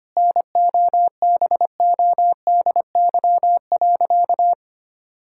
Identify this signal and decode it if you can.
Morse code, heard as a keyed tone, about 25 wpm, 700 Hz. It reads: NOBODY.
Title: Sentences from Top 1600 Words 25wpm